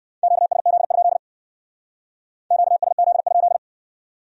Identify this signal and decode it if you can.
Morse code, heard as a keyed tone, about 54 wpm, 700 Hz. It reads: QSZ? QSZ?
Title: Speed-Racing Q-Codes - 45wpm